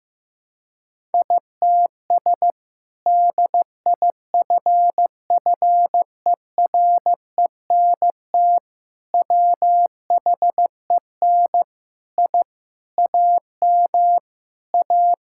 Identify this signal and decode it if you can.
Morse code, heard as a keyed tone, about 15 wpm, 700 Hz. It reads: ITS DIFFERENT WHEN I AM A
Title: Sentences from Top 1000 Words - Review Part 5 - 15wpm